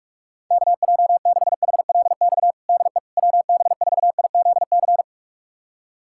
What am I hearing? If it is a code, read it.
Morse code, heard as a keyed tone, about 41 wpm, 700 Hz. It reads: KJ6HLX DE WB4IZC
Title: USA Callsign DE Callsign - 6 Characters - 45wpm